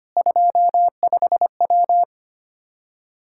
Morse code, heard as a keyed tone, about 25 wpm, 700 Hz. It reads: 25W
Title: K1USN SST QSO Practice 25wpm